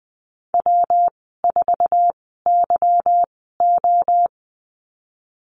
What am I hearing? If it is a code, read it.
Morse code, heard as a keyed tone, about 20 wpm, spaced as for 15 wpm, 700 Hz. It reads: W4YO
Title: 193 Lesson 31g - Letter v - Call Signs 20wpm